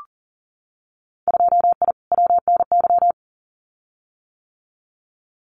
Morse code, heard as a keyed tone, about 40 wpm, 700 Hz. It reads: 2I WNY
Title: Winter Field Day Exchanges 40wpm